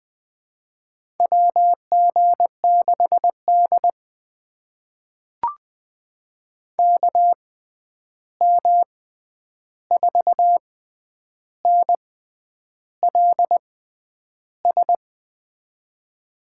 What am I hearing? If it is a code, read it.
Morse code, heard as a keyed tone, about 20 wpm, spaced as for 9 wpm, 700 Hz. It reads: WG6D KM4NLS
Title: USA Call Signs - 6x Character Spacing - 20wpm